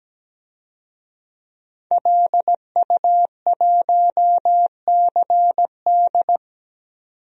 Morse code, heard as a keyed tone, about 17 wpm, 700 Hz. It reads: LU1CD
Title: DXCC-100 Call Signs - 5 Characters - 17wpm